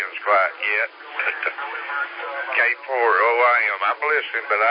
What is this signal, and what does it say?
Emergency Alert System (SAME) tones, heard as harsh buzzing bursts, with speech. It is heard from 0.5 to 4.4 s.